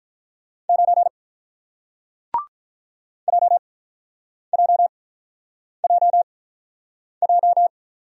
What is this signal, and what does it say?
Morse code, heard as a keyed tone, about 48 wpm, spaced as for 9 wpm, 690 Hz. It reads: 9 JJJJ